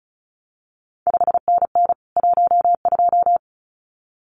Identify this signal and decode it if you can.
Morse code, heard as a keyed tone, about 36 wpm, 700 Hz. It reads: 5NN 12
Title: CQ WW DX Contest Exchange Encore - 35wpm